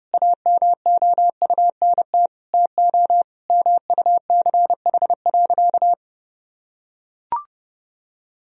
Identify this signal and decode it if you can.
Morse code, heard as a keyed tone, about 30 wpm, 700 Hz. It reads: AMOUNT TO MUCH.
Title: Sentences from Top 1000 Words - Review Part 4 - 30wpm